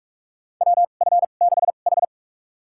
Morse code, heard as a keyed tone, about 44 wpm, spaced as for 32 wpm, 700 Hz. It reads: WF6H
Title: USA Call Signs - 4 Characters - 2x Character Spacing - 45wpm